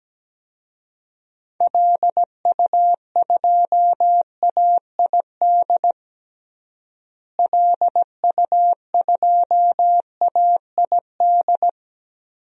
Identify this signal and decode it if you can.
Morse code, heard as a keyed tone, about 17 wpm, 700 Hz. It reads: LU2AID LU2AID
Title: DXCC-100 Call Signs - 6 Characters - Repeated 3x - 17wpm